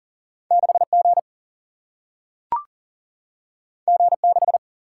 Morse code, heard as a keyed tone, about 38 wpm, 700 Hz. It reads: 6 G G 6
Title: Single-Letter Single-Number 40wpm